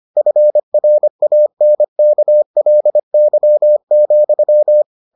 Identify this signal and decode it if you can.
Morse code, heard as a keyed tone, about 25 wpm, 600 Hz. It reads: FRANKLY,